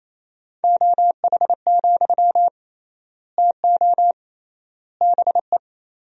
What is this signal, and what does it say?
Morse code, heard as a keyed tone, about 28 wpm, 700 Hz. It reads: OH, TO BE
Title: 3x Word-Spacing - Sentences from Top 500 Words - 28wpm